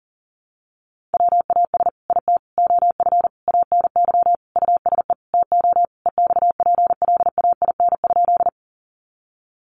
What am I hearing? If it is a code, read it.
Morse code, heard as a keyed tone, about 40 wpm, 700 Hz. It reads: WAS IT OF ANY USE TO EXPLAIN?
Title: Sentences from Top 1100 Words 40wpm